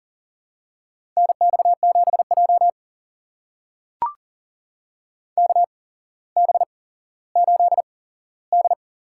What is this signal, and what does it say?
Morse code, heard as a keyed tone, about 40 wpm, spaced as for 17 wpm, 700 Hz. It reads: NX7J KB8D
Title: USA Call Signs - 8x Character Spacing - 40wpm